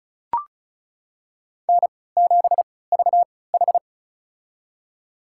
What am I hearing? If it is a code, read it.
Morse code, heard as a keyed tone, about 36 wpm, spaced as for 20 wpm, 700 Hz. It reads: N7VH
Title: USA Call Signs - 4 Characters - 3x Character Spacing - 35wpm